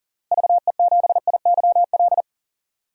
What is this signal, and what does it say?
Morse code, heard as a keyed tone, about 37 wpm, 700 Hz. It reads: VE7IYL
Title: DXCC-100 Call Signs - 6 Characters - Repeated 3x - 40wpm